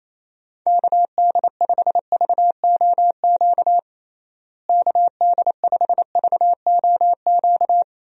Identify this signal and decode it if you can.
Morse code, heard as a keyed tone, about 28 wpm, 700 Hz. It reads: KD5VOQ KD5VOQ
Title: USA Call Signs - 6 Characters - Repeated 2x - 28wpm